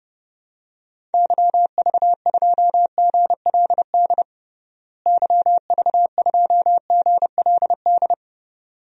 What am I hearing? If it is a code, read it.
Morse code, heard as a keyed tone, about 30 wpm, 700 Hz. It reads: YV2GLD YV2GLD
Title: DXCC-100 Call Signs - 6 Characters - Repeated 2x - 30wpm